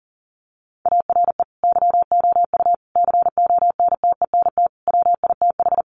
Morse code, heard as a keyed tone, about 40 wpm, 700 Hz. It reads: ARE YOU CONTENT WITH
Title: Sentences from Top 900 Words 40wpm